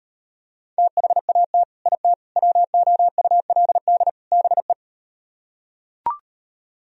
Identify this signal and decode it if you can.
Morse code, heard as a keyed tone, about 39 wpm, 690 Hz. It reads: THAT IT WOULD BE